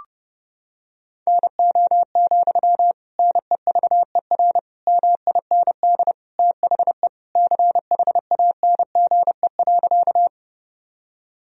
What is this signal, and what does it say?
Morse code, heard as a keyed tone, about 31 wpm, 700 Hz. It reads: NO, NEVER MIND THE CHANGE.
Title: Sentences from Top 1000 Words - Review Part 1 - 30wpm